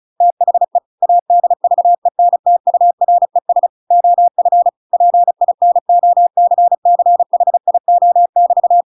Morse code, heard as a keyed tone, about 34 wpm, 700 Hz. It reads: THE ADVENTURES OF PINOCCHIO<BT>
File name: Pinocchio / Pinocchio ch7 35 35 ik1hge